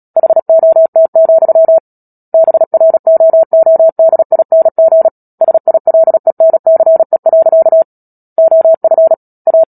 Morse code, heard as a keyed tone, about 37 wpm, 650 Hz. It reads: HOT, BROODING SILENCE. OF A
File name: Book Two / 10-Chapter 10 - Epilogue - 36 WPM